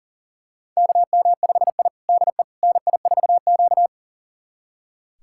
Morse code, heard as a keyed tone, about 40 wpm, 700 Hz. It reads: KM5I DE NI4Q
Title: USA Callsign DE Callsign - 4 Characters - 40wpm